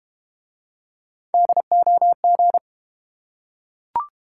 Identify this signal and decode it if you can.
Morse code, heard as a keyed tone, about 31 wpm, 700 Hz. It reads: DOG
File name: Easy Words - Speed-Racing with Hints - 25wpm